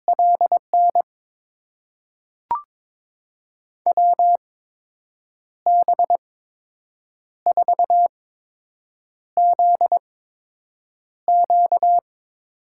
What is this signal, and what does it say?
Morse code, heard as a keyed tone, about 22 wpm, spaced as for 7 wpm, 700 Hz. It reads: LN WB4ZQ
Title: USA Call Signs - 8x Character Spacing - 22wpm